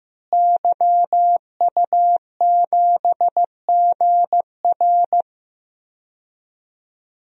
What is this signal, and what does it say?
Morse code, heard as a keyed tone, about 15 wpm, 700 Hz. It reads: YU7GR
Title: DXCC-100 Call Signs - 5 Characters - Repeated 2x - 15wpm